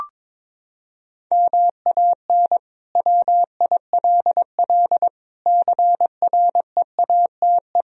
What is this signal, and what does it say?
Morse code, heard as a keyed tone, about 22 wpm, 700 Hz. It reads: MAN WILL CREATE
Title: Sentences from Top 1000 Words - Review Part 4 - 22wpm